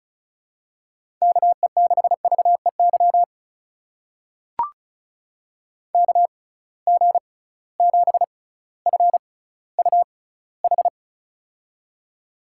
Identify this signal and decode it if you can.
Morse code, heard as a keyed tone, about 36 wpm, spaced as for 18 wpm, 690 Hz. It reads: KE6VEY KG7FUH
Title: USA Call Signs - 6 Characters - 6x Character Spacing - 35wpm